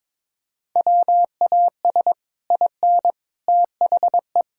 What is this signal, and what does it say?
Morse code, heard as a keyed tone, about 22 wpm, 700 Hz. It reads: WAS IN THE
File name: Sets of 9 Words - 22wpm